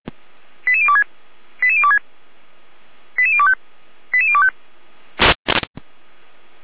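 Five-tone selective call (ZVEI), heard as quick runs of five tones, tones of about 70 ms. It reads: 88026 88026 88025 88025
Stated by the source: ZVEI five-tone selective call 88159, received by SDR.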